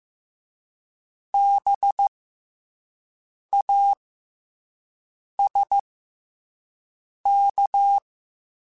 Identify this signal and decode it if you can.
Morse code, heard as a keyed tone, about 15 wpm, spaced as for 5 wpm, 800 Hz. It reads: BASK
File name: Special INGs-mp3s / Special INGs-mp3s / SlowToFaster ing @10